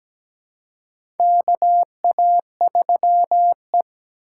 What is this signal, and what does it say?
Morse code, heard as a keyed tone, about 17 wpm, 700 Hz. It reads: KA3E
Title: USA Call Signs - 2x Character Spacing - 17wpm